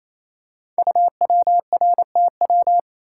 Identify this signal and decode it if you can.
Morse code, heard as a keyed tone, about 27 wpm, 700 Hz. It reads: UWRTW
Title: Code Groups - 5 Letters - 28wpm